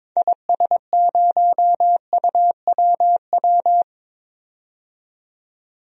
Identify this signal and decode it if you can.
Morse code, heard as a keyed tone, about 22 wpm, 700 Hz. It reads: IS0UWW
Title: DXCC-100 Call Signs - 6 Characters - 22wpm